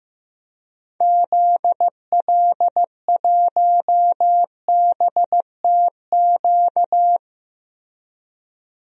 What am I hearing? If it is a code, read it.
Morse code, heard as a keyed tone, about 15 wpm, 700 Hz. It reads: ZL1BTQ
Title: DXCC-100 Call Signs - 6 Characters - Repeated 2x - 15wpm